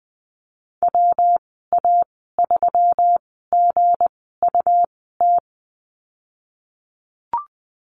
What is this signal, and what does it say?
Morse code, heard as a keyed tone, about 20 wpm, spaced as for 14 wpm, 700 Hz. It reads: WA3GUT
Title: 219 Lesson 35g - Number 0 - Call Signs 20wpm